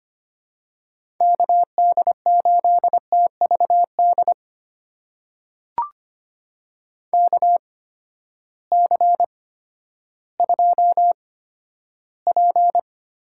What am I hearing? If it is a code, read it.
Morse code, heard as a keyed tone, about 25 wpm, spaced as for 12 wpm, 700 Hz. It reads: KD8TVD KC2P
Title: USA Call Signs - 6 Characters - 8x Character Spacing - 25wpm